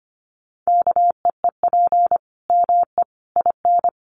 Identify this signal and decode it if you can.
Morse code, heard as a keyed tone, about 25 wpm, 700 Hz. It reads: KEEP ME IN
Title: Sentences from Top 900 Words 25wpm